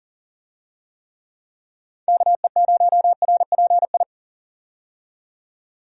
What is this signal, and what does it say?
Morse code, heard as a keyed tone, about 40 wpm, 700 Hz. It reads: KE0RPI
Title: USA Call Signs - 6 Characters - 40wpm